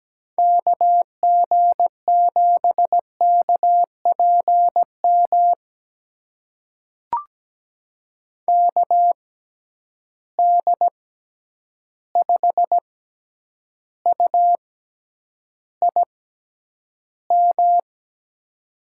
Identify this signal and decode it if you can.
Morse code, heard as a keyed tone, about 17 wpm, spaced as for 9 wpm, 700 Hz. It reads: KG7KPM KD5UIM
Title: USA Call Signs - 6 Characters - 6x Character Spacing - 17wpm